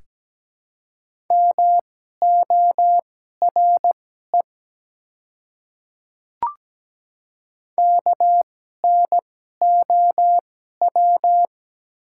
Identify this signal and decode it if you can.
Morse code, heard as a keyed tone, about 17 wpm, spaced as for 12 wpm, 700 Hz. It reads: MORE KNOW
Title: 2x Character Spacing - Top 200 Words - 17wpm